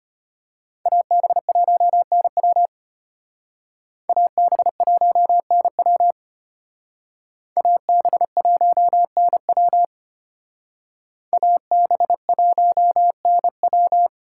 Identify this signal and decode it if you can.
Morse code, heard as a keyed tone, about 31 wpm, 700 Hz. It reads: AB1NW AB1NW AB1NW AB1NW
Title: Speed Racing - USA 2x2 Call Signs 25wpm